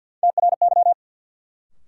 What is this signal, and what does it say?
Morse code, heard as a keyed tone, about 53 wpm, 700 Hz. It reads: TRY
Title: Sets of 2 Words - New Word Drill - 50wpm